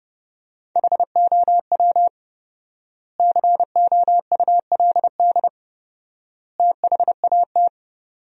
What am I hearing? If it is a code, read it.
Morse code, heard as a keyed tone, about 30 wpm, 700 Hz. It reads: HOW COULD THAT
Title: 4x Word-Spacing - Sentences from Top 400 Words - 30wpm